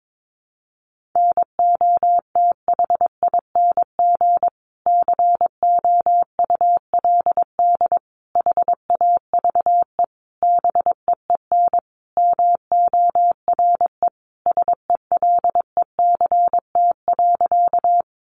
Morse code, heard as a keyed tone, about 22 wpm, 700 Hz. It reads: NOTHING COULD HAVE BEEN MORE SELECT.
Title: Sentences from Top 1300 Words 22wpm